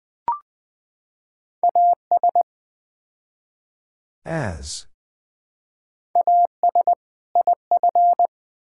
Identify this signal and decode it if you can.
Morse code, heard as a keyed tone, about 20 wpm, 700 Hz. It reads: AS AS IF
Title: Build A Phrase - 4-Word Phrases - 20wpm